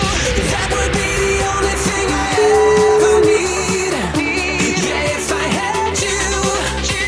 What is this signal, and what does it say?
Ringback tone (Europe, 425 Hz), with noise.